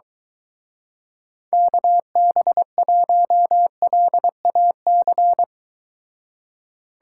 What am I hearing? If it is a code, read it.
Morse code, heard as a keyed tone, about 23 wpm, 700 Hz. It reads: KB1LAC